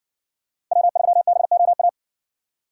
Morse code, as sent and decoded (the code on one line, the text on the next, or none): .-- ...-- -... --- .-
W3BOA